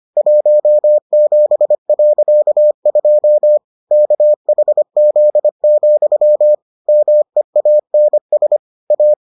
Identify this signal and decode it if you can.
Morse code, heard as a keyed tone, about 25 wpm, 600 Hz. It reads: 17.2 KHZ, MEANS A